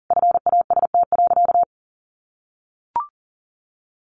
Morse code, heard as a keyed tone, about 40 wpm, 700 Hz. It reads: FAST.
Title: Sentences from Top 1200 Words 40wpm